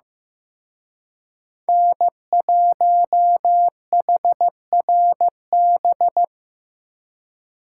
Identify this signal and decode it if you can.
Morse code, heard as a keyed tone, about 18 wpm, 700 Hz. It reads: N1HRB
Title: Speed Racing - USA 1x3 Call Signs 15wpm